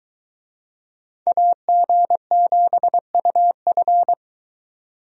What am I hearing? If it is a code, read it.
Morse code, heard as a keyed tone, about 21 wpm, 700 Hz. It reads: AG7UF